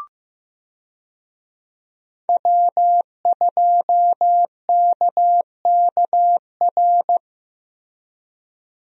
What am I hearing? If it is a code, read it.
Morse code, heard as a keyed tone, about 15 wpm, 700 Hz. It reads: W2KKR
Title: USA Call Signs - 5 Characters - Repeated 2x - 15wpm